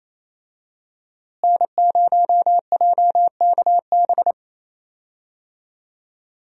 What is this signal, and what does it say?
Morse code, heard as a keyed tone, about 28 wpm, 700 Hz. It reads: N0JKB
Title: USA Call Signs - 5 Characters - Repeated 2x - 28wpm